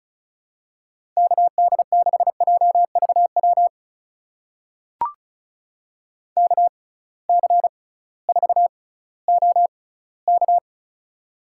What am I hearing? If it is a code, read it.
Morse code, heard as a keyed tone, about 36 wpm, spaced as for 19 wpm, 700 Hz. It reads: KD6JVW KC4OK
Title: USA Call Signs - 6x Character Spacing - 35wpm